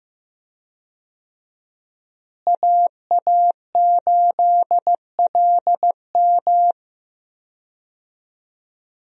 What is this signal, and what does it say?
Morse code, heard as a keyed tone, about 16 wpm, 700 Hz. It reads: AA8LM
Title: USA Call Signs - 5 Characters - Repeated 2x - 15wpm